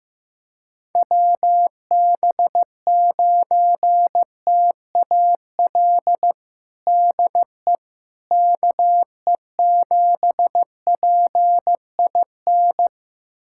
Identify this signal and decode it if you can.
Morse code, heard as a keyed tone, about 15 wpm, 700 Hz. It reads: WB9TAL DE KE7PIN
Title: USA Callsign DE Callsign - 6 Characters - 15wpm